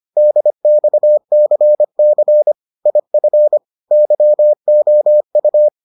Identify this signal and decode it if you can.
Morse code, heard as a keyed tone, about 25 wpm, 600 Hz. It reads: DXCC IF YOU